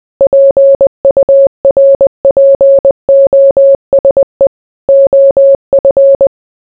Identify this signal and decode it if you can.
Morse code, heard as a keyed tone, about 20 wpm, 550 Hz. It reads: PURPOSE OF